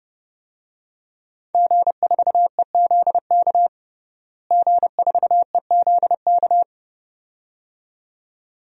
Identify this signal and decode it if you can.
Morse code, heard as a keyed tone, about 31 wpm, 700 Hz. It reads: G4EZK G4EZK
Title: DXCC-100 Call Signs - 5 Characters - Repeated 2x - 30wpm